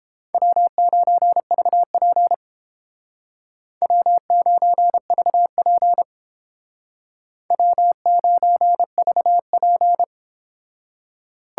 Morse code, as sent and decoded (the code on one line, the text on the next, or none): .-- ----. ...- .--. / .-- ----. ...- .--. / .-- ----. ...- .--.
W9VP W9VP W9VP